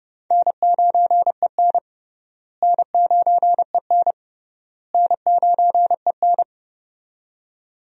Morse code, heard as a keyed tone, about 30 wpm, 700 Hz. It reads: N9EN N9EN N9EN
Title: USA Call Signs - 4 Characters - Repeated 3x - 30wpm